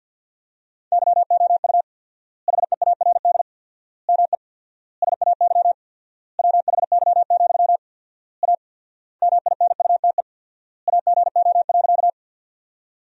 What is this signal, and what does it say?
Morse code, heard as a keyed tone, about 50 wpm, 700 Hz. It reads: YOU HEARD ME SAY WHY, A MINUTE AGO.
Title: 4x Word-Spacing - Sentences from Top 500 Words - 50wpm